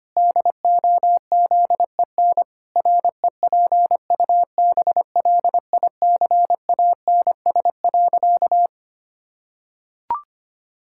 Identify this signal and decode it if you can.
Morse code, heard as a keyed tone, about 25 wpm, 700 Hz. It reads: DOZEN REPUBLICANS.
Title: Pangram Challenge v2 - 25wpm